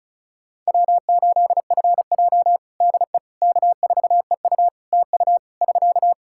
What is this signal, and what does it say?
Morse code, heard as a keyed tone, about 36 wpm, 700 Hz. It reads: W8FJ DE K4EU TU <SK>
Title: Practice QSO - Exchange 4 - Ending - 35wpm